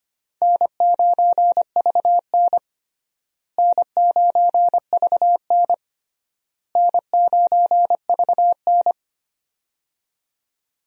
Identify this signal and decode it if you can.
Morse code, heard as a keyed tone, about 25 wpm, 700 Hz. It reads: N9VN N9VN N9VN
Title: USA Call Signs - 4 Characters - Repeated 3x - 25wpm